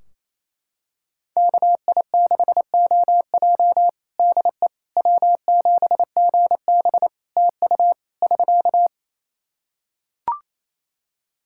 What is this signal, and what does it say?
Morse code, heard as a keyed tone, about 27 wpm, 700 Hz. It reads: KI6OJ DE W7GB TU <SK>
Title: Practice QSO - Exchange 4 - Ending - 28wpm